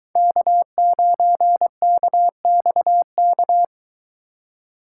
Morse code, heard as a keyed tone, about 23 wpm, 700 Hz. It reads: K9KXK